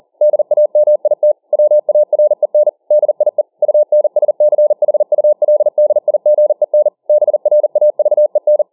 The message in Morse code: DAMIT WAREN DIE UNSCHULDIGEN BRAVEN

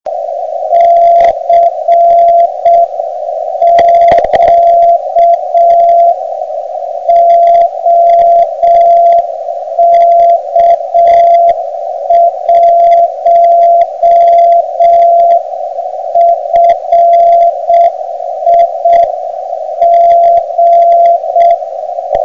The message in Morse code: CQ OK1FIG

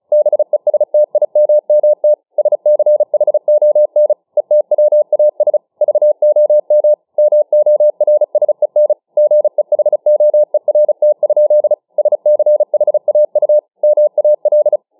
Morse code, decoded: BESTIMMT SCHON ETWAS VOM MORSEN GEHOERT? SCHAU MAL

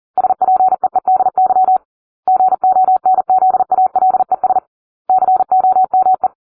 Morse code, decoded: SPEEDY GONZALES COME

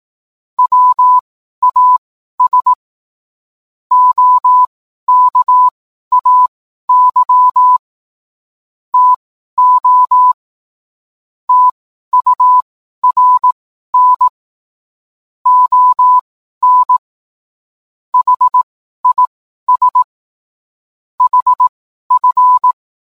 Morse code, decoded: WAS OKAY TO TURN ON HIS HF

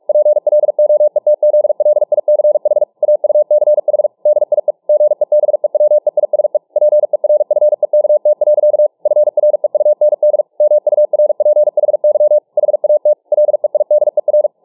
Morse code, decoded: PLOETZLICH AUCH DIE GEBEWEISE PERFEKT. FREUND MURPHY HAT LEIDER

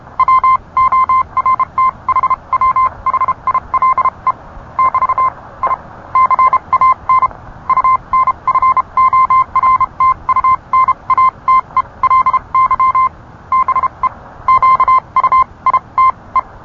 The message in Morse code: WORTHWHILE <BT> I CAN UNFORTUNATELY BE QUITE

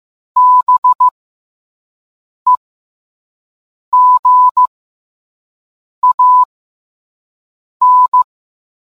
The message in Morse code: BEGAN

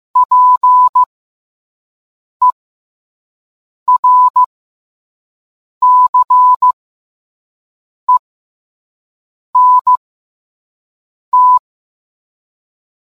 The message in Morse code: PERCENT

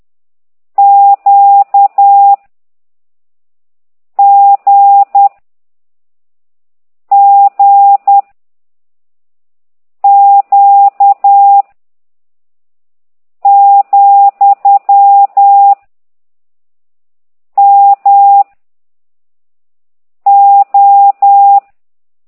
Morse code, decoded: Q G G Q , M O

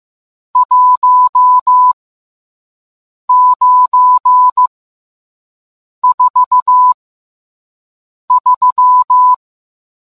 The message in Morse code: 1943

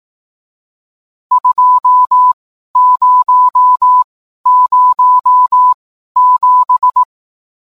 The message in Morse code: 2007

